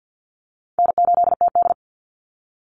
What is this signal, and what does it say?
Morse code, heard as a keyed tone, about 50 wpm, 700 Hz. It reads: N8TD
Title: Call Signs Repeated 3x - 50wpm